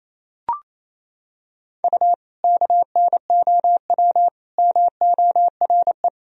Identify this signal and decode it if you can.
Morse code, heard as a keyed tone, about 28 wpm, 700 Hz. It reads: U KNOW MORE
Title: Sentences from Top 100 Words and Abbreviations 28wpm